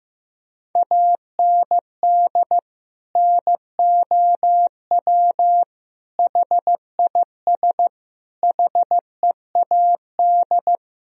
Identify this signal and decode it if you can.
Morse code, heard as a keyed tone, about 14 wpm, 700 Hz. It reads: AND NOW HIS HEAD
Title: Sentences from Top 1000 Words - Review Part 1 - 15wpm